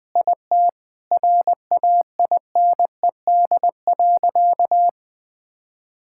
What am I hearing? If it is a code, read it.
Morse code, heard as a keyed tone, about 20 wpm, 700 Hz. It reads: IT RAINED.
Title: Sentences with 3-Word Phrases - 20wpm